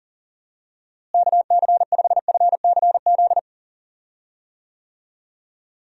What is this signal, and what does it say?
Morse code, heard as a keyed tone, about 40 wpm, 700 Hz. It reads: KC5FCZ
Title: USA Call Signs - 6 Characters - Repeated 2x - 40wpm